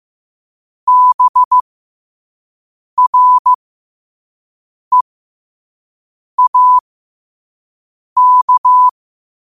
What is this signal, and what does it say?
Morse code, heard as a keyed tone, about 15 wpm, spaced as for 5 wpm, 1000 Hz. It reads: BREAK
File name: qotdfiles / 5wpm